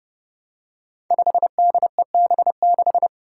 Morse code, heard as a keyed tone, about 30 wpm, 700 Hz. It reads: 5DEB6
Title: Five Challenging Letter-Number Combinations 30wpm